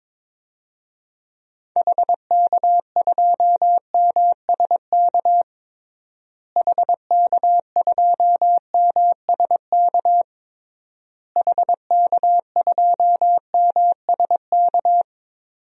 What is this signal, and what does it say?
Morse code, heard as a keyed tone, about 22 wpm, 700 Hz. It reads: HK2MSK HK2MSK HK2MSK
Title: DXCC-100 Call Signs - 6 Characters - Repeated 3x - 22wpm